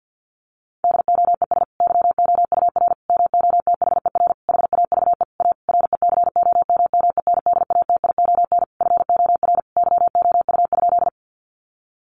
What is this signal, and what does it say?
Morse code, heard as a keyed tone, about 50 wpm, 700 Hz. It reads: DOES YOUR MOTHER HAVE A RECOMMENDATION FOR YOU?
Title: Sentences from Top 1400 Words 50wpm